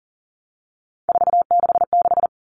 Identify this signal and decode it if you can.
Morse code, heard as a keyed tone, about 40 wpm, 700 Hz. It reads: 466
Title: All Cut Numbers - 3-Digit 40wpm